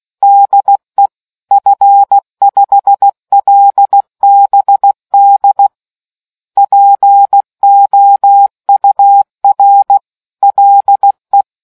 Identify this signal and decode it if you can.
Morse code, heard as a keyed tone, about 16 wpm, 800 Hz. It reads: DE F5LBD POUR LE